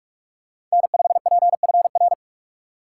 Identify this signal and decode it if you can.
Morse code, heard as a keyed tone, about 44 wpm, 700 Hz. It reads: N5PFR
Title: USA Call Signs - 5 Characters - 6x Character Spacing - 45wpm